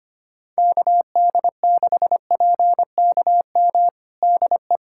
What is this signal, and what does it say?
Morse code, heard as a keyed tone, about 25 wpm, 700 Hz. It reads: KD6PKM DE
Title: USA Callsign DE Callsign 25wpm